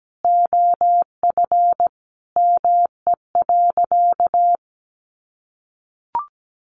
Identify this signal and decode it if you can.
Morse code, heard as a keyed tone, about 17 wpm, 700 Hz. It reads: OF ME.
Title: Sentences from Top 300 Words 17wpm